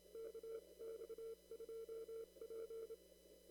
Morse code, heard as a keyed tone, about 27 wpm, spaced as for 23 wpm, 450 Hz. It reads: KX2P
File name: CW / kt5j wpx cw-op-k5pi